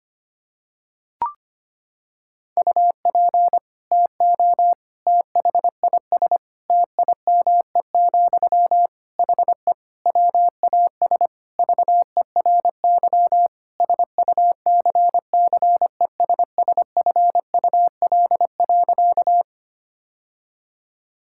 Morse code, heard as a keyed tone, about 25 wpm, 700 Hz. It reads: UP TO THIS TIME, HE WAS VERY SUCCESSFUL.
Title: Sentences from Top 1000 Words - Review Part 6 - 25wpm